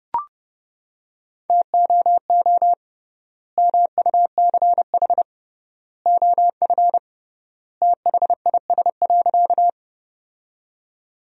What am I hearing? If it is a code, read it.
Morse code, heard as a keyed tone, about 30 wpm, 700 Hz. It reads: TOO MUCH OF THIS.
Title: 3x Word-Spacing - Sentences from Top 200 Words - 30wpm